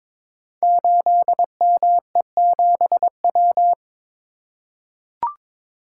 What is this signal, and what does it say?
Morse code, heard as a keyed tone, about 23 wpm, 690 Hz. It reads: 8ME7W
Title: Code Groups - 5 Letter:Numbers - 22wpm